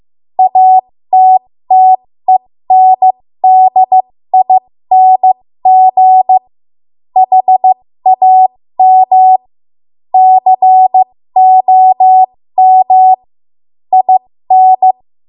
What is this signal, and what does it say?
Morse code, heard as a keyed tone, about 15 wpm, spaced as for 12 wpm, 750 Hz. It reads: ATTENDING HAMCOM IN
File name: Morse / 13WPM